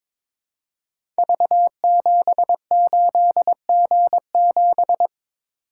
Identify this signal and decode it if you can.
Morse code, heard as a keyed tone, about 22 wpm, 700 Hz. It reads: V78G7
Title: Code Groups - 5 Letter:Numbers - 22wpm